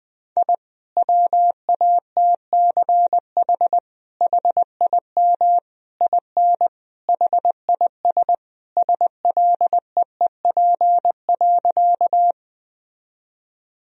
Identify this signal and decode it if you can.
Morse code, heard as a keyed tone, about 20 wpm, 700 Hz. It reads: I WATCH HIM IN HIS SLEEP.
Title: Sentences from Top 1000 Words - Review Part 1 - 20wpm